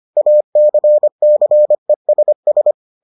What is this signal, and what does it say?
Morse code, heard as a keyed tone, about 25 wpm, 600 Hz. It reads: ACCESS